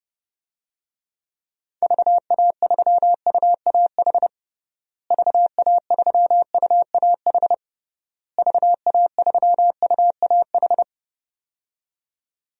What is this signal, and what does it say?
Morse code, heard as a keyed tone, about 32 wpm, 700 Hz. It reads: VA3UAH VA3UAH VA3UAH
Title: DXCC-100 Call Signs - 6 Characters - Repeated 3x - 30wpm